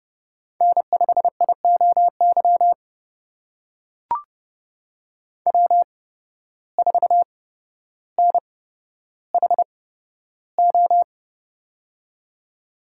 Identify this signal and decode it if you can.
Morse code, heard as a keyed tone, about 31 wpm, spaced as for 13 wpm, 700 Hz. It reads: N5IOY W4NHO
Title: USA Call Signs - 5 Characters - 8x Character Spacing - 30wpm